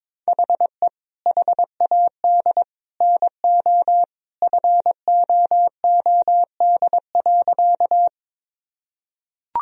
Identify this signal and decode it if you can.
Morse code, heard as a keyed tone, about 21 wpm, 700 Hz. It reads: HE HAD NO FOOD.
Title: Sentences from Top 1000 Words - Review Part 1 - 22wpm